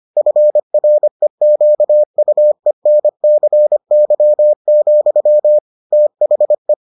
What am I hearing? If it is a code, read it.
Morse code, heard as a keyed tone, about 25 wpm, 600 Hz. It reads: FREQUENCY, THE